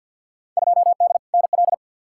Morse code, heard as a keyed tone, about 51 wpm, 700 Hz. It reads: 2D NL
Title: Field Day Exchanges Encore - 50wpm